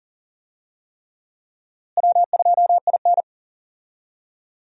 Morse code, heard as a keyed tone, about 40 wpm, 700 Hz. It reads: W2IN
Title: USA Call Signs - 4 Characters - Repeated 3x - 40wpm